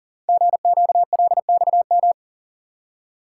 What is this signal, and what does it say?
Morse code, heard as a keyed tone, about 42 wpm, 700 Hz. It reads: GQLXM
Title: Code Groups - 5 Letters - 40wpm